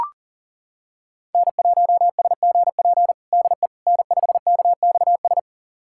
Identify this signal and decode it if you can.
Morse code, heard as a keyed tone, about 40 wpm, 700 Hz. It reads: N1SGP DE N5KXS
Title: USA Callsign DE Callsign - 5 Characters - 40wpm